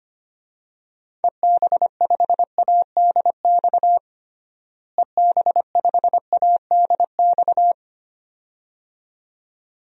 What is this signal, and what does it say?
Morse code, heard as a keyed tone, about 25 wpm, 700 Hz. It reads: EB5ADX EB5ADX
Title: DXCC-100 Call Signs - 6 Characters - Repeated 2x - 25wpm